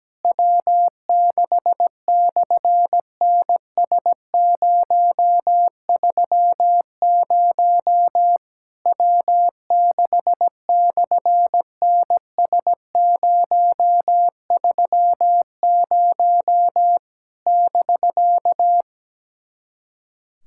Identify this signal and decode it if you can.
Morse code, heard as a keyed tone, about 17 wpm, 700 Hz. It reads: W6/NS030 W6/NS030 <BK>
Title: SOTA QSO Practice 17wpm